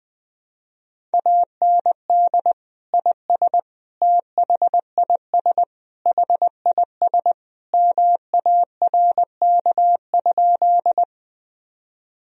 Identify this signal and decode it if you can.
Morse code, heard as a keyed tone, about 20 wpm, 700 Hz. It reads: AND IS THIS HIS MARK?
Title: Sentences from Top 1000 Words - Review Part 1 - 20wpm